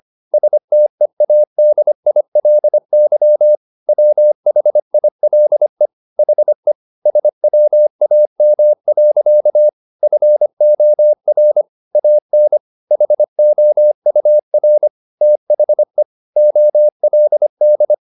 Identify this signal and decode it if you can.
Morse code, heard as a keyed tone, about 25 wpm, 600 Hz. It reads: STEADILY WHILE HE SWAM. FOR AN HOUR THE OLD